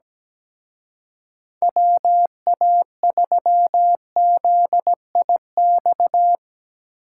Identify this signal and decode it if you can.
Morse code, heard as a keyed tone, about 20 wpm, 700 Hz. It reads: WA3ZIX
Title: Speed Racing - USA Call Signs 17wpm